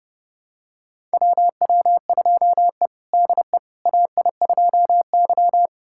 Morse code, heard as a keyed tone, about 32 wpm, 700 Hz. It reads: WW2E DE AI2Y
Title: USA Callsign DE Callsign - 4 Characters - 30wpm